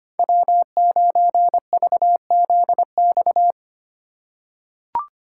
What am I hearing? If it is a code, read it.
Morse code, heard as a keyed tone, about 25 wpm, 700 Hz. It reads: W9VZX
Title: USA Call Signs - 4x Character Spacing - 25wpm